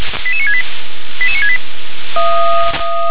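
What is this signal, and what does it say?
Five-tone selective call (ZVEI), heard as quick runs of five tones, tones of about 70 ms. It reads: 88968 88968
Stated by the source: ZVEI five-tone selective call received on SDR.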